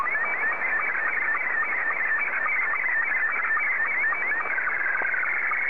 SSTV (slow-scan television), Scottie 2.